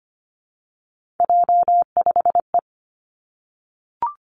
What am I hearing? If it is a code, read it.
Morse code, heard as a keyed tone, about 25 wpm, 700 Hz. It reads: J5E
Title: Three Challenging Letter-Number Combinations 25wpm